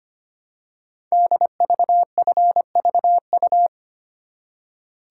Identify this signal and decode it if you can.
Morse code, heard as a keyed tone, about 25 wpm, 700 Hz. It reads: DVFVU
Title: Code Groups - 5 Letters - 25wpm